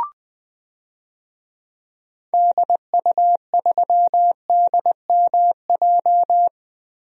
Morse code, heard as a keyed tone, about 20 wpm, 700 Hz. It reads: DU3DMJ
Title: DXCC-100 Call Signs - 6 Characters - Repeated 2x - 20wpm